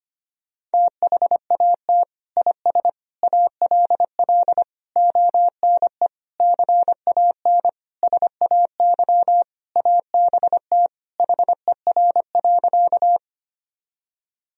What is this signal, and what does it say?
Morse code, heard as a keyed tone, about 25 wpm, 700 Hz. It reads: THAT IS ALL ONE CAN SAY ABT HER.
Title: Sentences from Top 200 Words and Abbreviations 25wpm